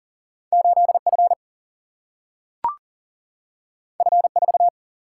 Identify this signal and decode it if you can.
Morse code, heard as a keyed tone, about 40 wpm, 700 Hz. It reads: 8F F4
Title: Single-Letter Single-Number 40wpm